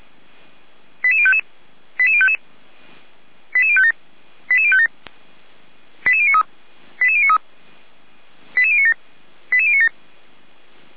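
Five-tone selective call (ZVEI), heard as quick runs of five tones, tones of about 70 ms. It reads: 88055 88055 88057 88057 88093 88093 88097 88097
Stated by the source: Five-tone ZVEI selective call received by SDR, on the audio server of feuerwehr-orlamuende.org.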